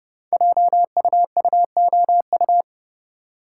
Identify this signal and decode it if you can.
Morse code, heard as a keyed tone, about 29 wpm, 700 Hz. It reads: JUUOU
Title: Code Groups - 5 Letters - 30wpm